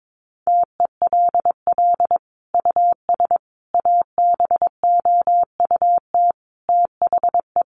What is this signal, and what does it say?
Morse code, heard as a keyed tone, about 22 wpm, 700 Hz. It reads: TELL US ABOUT THE
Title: Sentences from Top 1300 Words 22wpm